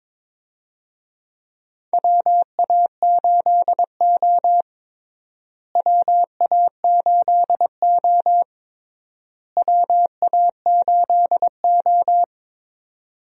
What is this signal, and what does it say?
Morse code, heard as a keyed tone, about 22 wpm, 700 Hz. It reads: WA8O WA8O WA8O
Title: USA Call Signs - 4 Characters - Repeated 3x - 22wpm